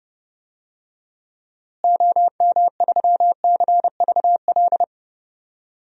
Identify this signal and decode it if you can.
Morse code, heard as a keyed tone, about 32 wpm, 700 Hz. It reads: OM3CVL
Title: DXCC-100 Call Signs - 6 Characters - Repeated 3x - 30wpm